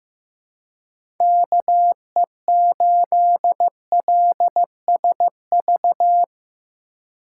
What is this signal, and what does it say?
Morse code, heard as a keyed tone, about 15 wpm, 700 Hz. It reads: KE8LSV
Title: USA Call Signs - 3x Character Spacing - 15wpm